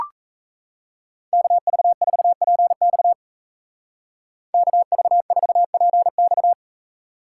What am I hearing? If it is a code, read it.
Morse code, heard as a keyed tone, about 35 wpm, 700 Hz. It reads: KV4PX KV4PX
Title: Speed Racing - USA 2x2 Call Signs 28wpm